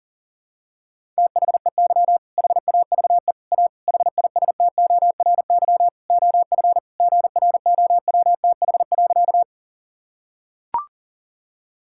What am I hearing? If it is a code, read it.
Morse code, heard as a keyed tone, about 40 wpm, 700 Hz. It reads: THEY HAVE A HISTORY OF GROWTH.
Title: Sentences from Top 1000 Words - Review Part 5 - 40wpm